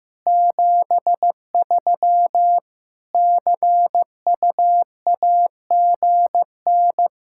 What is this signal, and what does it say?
Morse code, heard as a keyed tone, about 15 wpm, 700 Hz. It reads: 73 CUAGN
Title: Practice QSO - Exchange 4 - Ending - 15wpm